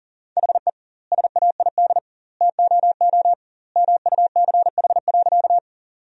Morse code, heard as a keyed tone, about 39 wpm, 700 Hz. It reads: HE SAID TOO MUCH.
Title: 2x Word-Spacing - Sentences from Top 200 Words - 40wpm